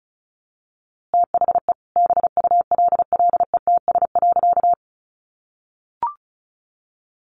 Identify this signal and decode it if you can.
Morse code, heard as a keyed tone, about 34 wpm, 700 Hz. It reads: THE BULLETS.
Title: A Sound Of Thunder 35wpm